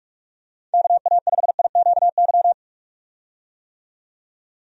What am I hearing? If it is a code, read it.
Morse code, heard as a keyed tone, about 46 wpm, 700 Hz. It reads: KA5IQY
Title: USA Call Signs - 6 Characters - 45wpm